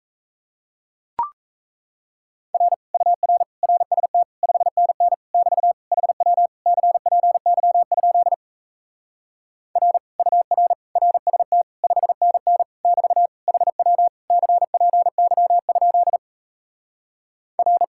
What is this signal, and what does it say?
Morse code, heard as a keyed tone, about 37 wpm, 700 Hz. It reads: R UR RST 5NN <BT> HW CPY? R UR RST 5NN <BT> HW CPY? R